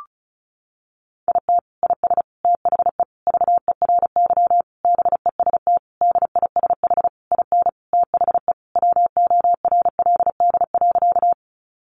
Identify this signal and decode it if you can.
Morse code, heard as a keyed tone, about 34 wpm, 700 Hz. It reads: IT IS THE VERY BEST DISH IN THE WORLD.
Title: Sentences from Top 1300 Words 35wpm